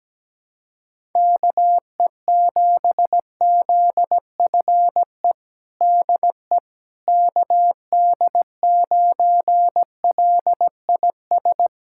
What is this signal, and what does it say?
Morse code, heard as a keyed tone, about 17 wpm, 700 Hz. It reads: KE7ZFE DE KD9LIS
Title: USA Callsign DE Callsign 17wpm